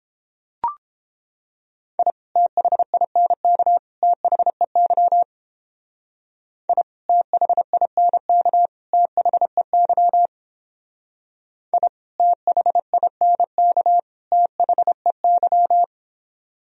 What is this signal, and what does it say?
Morse code, heard as a keyed tone, about 31 wpm, 700 Hz. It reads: I THINK THEY I THINK THEY I THINK THEY